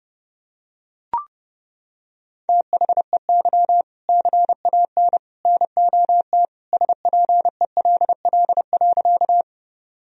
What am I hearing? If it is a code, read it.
Morse code, heard as a keyed tone, about 32 wpm, 700 Hz. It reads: THEY CAN NOT SPELL.
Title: Sentences from Top 1000 Words - Review Part 1 - 30wpm